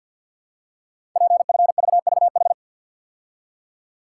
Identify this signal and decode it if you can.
Morse code, heard as a keyed tone, about 52 wpm, 700 Hz. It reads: PFVVH
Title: Code Groups - 5 Letters - 50wpm